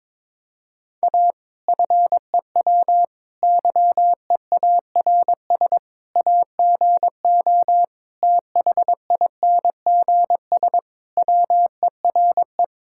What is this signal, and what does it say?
Morse code, heard as a keyed tone, about 21 wpm, 700 Hz. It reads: A FEW YEARS AGO THINGS WERE
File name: practice / 22wpm / Sentences with 4-Word Phrases - 22wpm